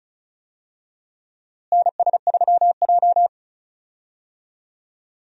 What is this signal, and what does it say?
Morse code, heard as a keyed tone, about 34 wpm, 700 Hz. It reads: NS3J
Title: USA Call Signs - 4 Characters - Repeated 3x - 35wpm